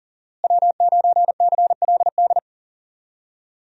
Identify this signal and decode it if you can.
Morse code, heard as a keyed tone, about 40 wpm, 700 Hz. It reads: W9CLD
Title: USA Call Signs - 5 Characters - Repeated 2x - 40wpm